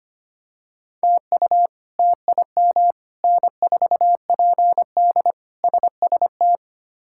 Morse code, heard as a keyed tone, about 25 wpm, 700 Hz. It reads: TU TIM N4PD SST
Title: K1USN SST QSO Practice - 25wpm